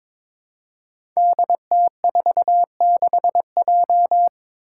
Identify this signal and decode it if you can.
Morse code, heard as a keyed tone, about 22 wpm, 700 Hz. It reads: DT46J
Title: Five Challenging Letter-Number Combinations 22wpm